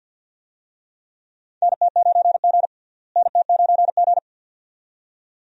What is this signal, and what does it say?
Morse code, heard as a keyed tone, about 47 wpm, 700 Hz. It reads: NT9G NT9G
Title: USA Call Signs - 4 Characters - Repeated 2x - 50wpm